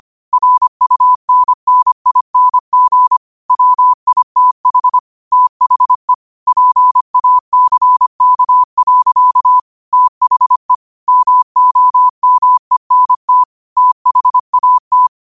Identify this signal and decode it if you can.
Morse code, heard as a keyed tone, about 25 wpm, 1000 Hz. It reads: RUNNING WITH THE PACK. THE MOMENT THAT